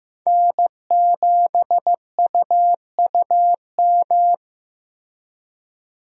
Morse code, heard as a keyed tone, about 15 wpm, 700 Hz. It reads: N7UUM
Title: USA Call Signs - 5 Characters - Repeated 3x - 15wpm